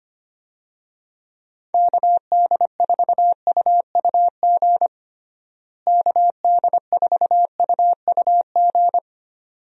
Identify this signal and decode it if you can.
Morse code, heard as a keyed tone, about 25 wpm, 700 Hz. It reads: KD4UUG KD4UUG
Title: USA Call Signs - 6 Characters - Repeated 2x - 25wpm